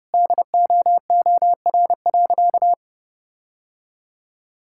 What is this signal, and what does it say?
Morse code, heard as a keyed tone, about 30 wpm, 700 Hz. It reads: DOOR.
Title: Sentences from Top 1600 Words 30wpm